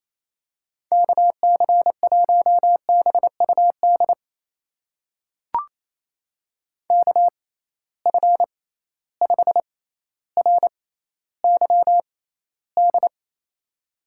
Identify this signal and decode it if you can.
Morse code, heard as a keyed tone, about 28 wpm, spaced as for 14 wpm, 700 Hz. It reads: KC1BUD KF5RYD
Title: USA Call Signs - 6 Characters - 6x Character Spacing - 28wpm